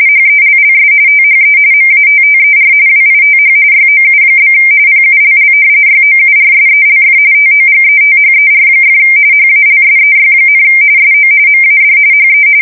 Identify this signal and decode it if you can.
RTTY, about 45.45 baud. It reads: HELLS THE MATTER WITH YOU? PULL YOURSELF TOGETHER! HALSEY, SO MAD HE
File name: fsk170 / battleshipnj